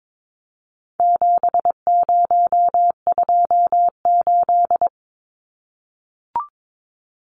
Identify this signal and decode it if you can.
Morse code, heard as a keyed tone, about 22 wpm, 700 Hz. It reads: 7028
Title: Numbers - 4-Digit 22wpm